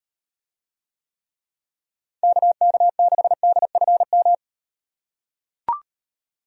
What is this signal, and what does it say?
Morse code, heard as a keyed tone, about 38 wpm, 700 Hz. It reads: KK6DFM